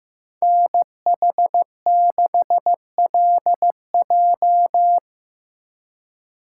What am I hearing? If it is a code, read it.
Morse code, heard as a keyed tone, about 18 wpm, 700 Hz. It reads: NH6LJ
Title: Speed Racing - USA 2x2 Call Signs 15wpm